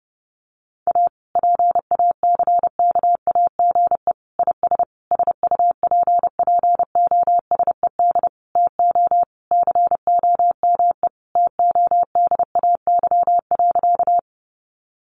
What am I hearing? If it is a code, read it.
Morse code, heard as a keyed tone, about 30 wpm, 700 Hz. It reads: A PACKAGE IS SUPPOSED TO COME TODAY.
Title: Easy Phrases 30wpm